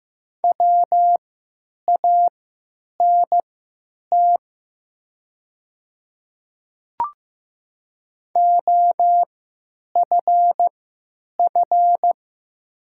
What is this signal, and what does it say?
Morse code, heard as a keyed tone, about 15 wpm, spaced as for 9 wpm, 700 Hz. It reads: WANT OFF
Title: 3x Character Spacing - Top 200 Words - 15wpm